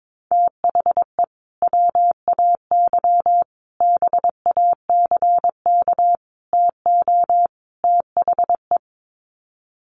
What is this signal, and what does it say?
Morse code, heard as a keyed tone, about 22 wpm, 700 Hz. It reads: THE WAY BACK TO THE
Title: Sets of 5 Words Encore2 22wpm